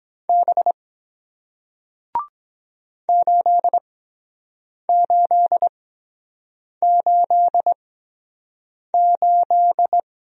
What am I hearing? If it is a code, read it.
Morse code, heard as a keyed tone, about 23 wpm, spaced as for 7 wpm, 700 Hz. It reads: B 8888